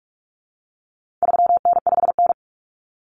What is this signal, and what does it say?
Morse code, heard as a keyed tone, about 44 wpm, 700 Hz. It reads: 3N5N
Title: Common Cut Numbers - 4-Digit 45wpm